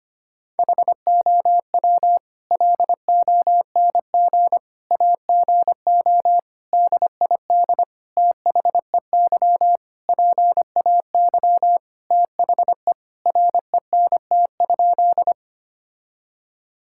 Morse code, heard as a keyed tone, about 25 wpm, 700 Hz. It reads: HOW LONG AGO DID THEY PAY THE RENT?
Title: Sentences from Top 1000 Words - Review Part 5 - 25wpm